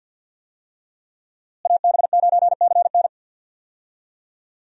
Morse code, heard as a keyed tone, about 48 wpm, 700 Hz. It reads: AB9CN
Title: USA Call Signs - 5 Characters - Repeated 3x - 50wpm